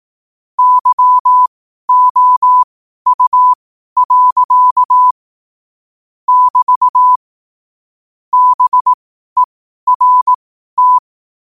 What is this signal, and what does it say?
Morse code, heard as a keyed tone, about 18 wpm, spaced as for 12 wpm, 1000 Hz. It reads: YOU. <BT> BERT
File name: qotdfiles / 12wpm